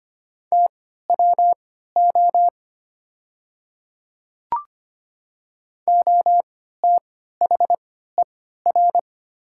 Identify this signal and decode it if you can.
Morse code, heard as a keyed tone, about 25 wpm, spaced as for 14 wpm, 700 Hz. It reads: TWO OTHER
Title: 3x Character Spacing - Top 100 Words - 25wpm